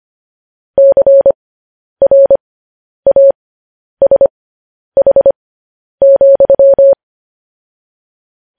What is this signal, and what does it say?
Morse code, heard as a keyed tone, about 25 wpm, spaced as for 10 wpm, 550 Hz. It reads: CRASH,
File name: podcast / News headlines-1x-25x10